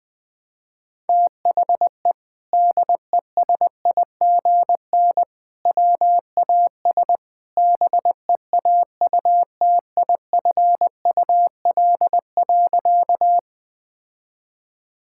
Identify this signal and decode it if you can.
Morse code, heard as a keyed tone, about 20 wpm, 700 Hz. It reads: THE DESIGN WAS BEAUTIFUL.
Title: Sentences from Top 1000 Words - Review Part 5 - 20wpm